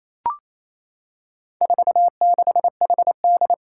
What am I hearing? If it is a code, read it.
Morse code, heard as a keyed tone, about 27 wpm, 700 Hz. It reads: 46HD
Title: Four Challenging Letter-Number Combinations 28wpm